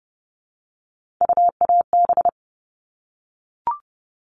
Morse code, heard as a keyed tone, about 30 wpm, 700 Hz. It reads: UAB
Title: All Cut Numbers - 3-Digit 30wpm